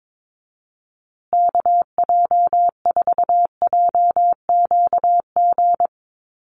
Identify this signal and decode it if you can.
Morse code, heard as a keyed tone, about 22 wpm, 700 Hz. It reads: KJ4JQG
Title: USA Call Signs Repeated 3x 22wpm